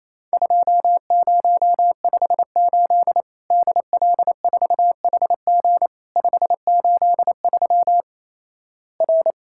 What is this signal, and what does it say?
Morse code, heard as a keyed tone, about 27 wpm, 700 Hz. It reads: 2058 DL4HG 583 R
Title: Worked All Europe DX Contest - QTC Exchange - 28wpm